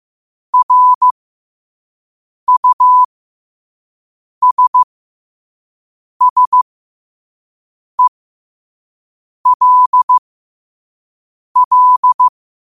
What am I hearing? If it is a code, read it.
Morse code, heard as a keyed tone, about 15 wpm, spaced as for 5 wpm, 1000 Hz. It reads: RUSSELL
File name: qotdfiles / 5wpm